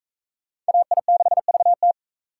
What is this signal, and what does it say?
Morse code, heard as a keyed tone, about 38 wpm, 700 Hz. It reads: AI6VT